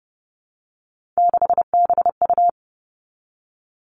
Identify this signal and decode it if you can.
Morse code, heard as a keyed tone, about 32 wpm, 700 Hz. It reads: 6BU
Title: Three Challenging Letter-Number Combinations 30wpm